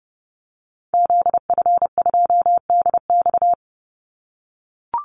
Morse code, heard as a keyed tone, about 30 wpm, 700 Hz. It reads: ZF2DX
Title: Call Signs - 30wpm